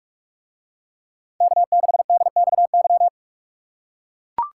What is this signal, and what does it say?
Morse code, heard as a keyed tone, about 43 wpm, 700 Hz. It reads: K6DXY